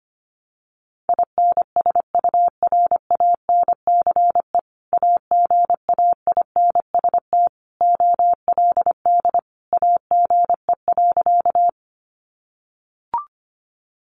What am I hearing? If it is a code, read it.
Morse code, heard as a keyed tone, about 25 wpm, 700 Hz. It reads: INSURANCE AGAINST OLD AGE.
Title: Sentences from Top 1100 Words 25wpm